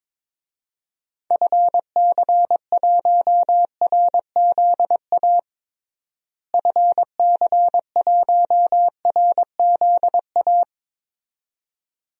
Morse code, heard as a keyed tone, about 22 wpm, 700 Hz. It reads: FC1RZA FC1RZA
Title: DXCC-100 Call Signs - 6 Characters - Repeated 2x - 22wpm